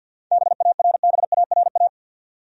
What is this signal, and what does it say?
Morse code, heard as a keyed tone, about 53 wpm, 700 Hz. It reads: BARBARA
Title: USA Common Names - 50wpm